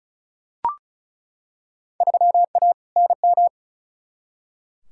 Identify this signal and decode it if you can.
Morse code, heard as a keyed tone, about 36 wpm, 700 Hz. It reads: 3A NM
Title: Field Day Exchanges Encore - 35wpm